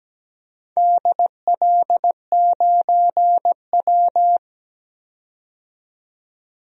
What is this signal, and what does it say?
Morse code, heard as a keyed tone, about 17 wpm, 700 Hz. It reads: DL9W
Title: DXCC-100 Call Signs - 4 Characters - 17wpm